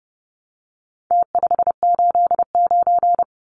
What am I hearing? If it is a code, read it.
Morse code, heard as a keyed tone, about 30 wpm, 700 Hz. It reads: T589
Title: Numbers - Contest Style - 4-Digit 30wpm